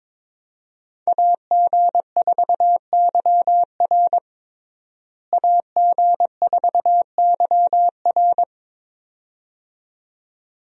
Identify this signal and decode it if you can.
Morse code, heard as a keyed tone, about 22 wpm, 700 Hz. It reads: AG4YR AG4YR
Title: USA Call Signs - 5 Characters - Repeated 2x - 22wpm